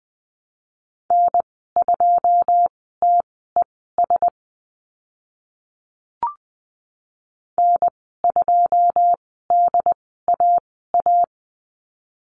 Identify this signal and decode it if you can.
Morse code, heard as a keyed tone, about 20 wpm, spaced as for 14 wpm, 700 Hz. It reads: N2TES N2DAA
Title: 082 Lesson 14g - Number 2 - Call Signs 20wpm